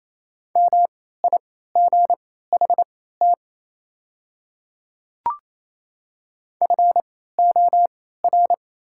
Morse code, heard as a keyed tone, about 27 wpm, spaced as for 16 wpm, 690 Hz. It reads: MIGHT FOR
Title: 3x Character Spacing - Top 300 Words - 28wpm